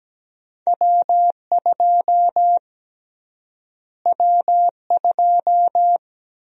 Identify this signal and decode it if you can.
Morse code, heard as a keyed tone, about 17 wpm, 700 Hz. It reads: W2 W2
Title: USA Call Sign Prefixes - Repeated 2x - 17wpm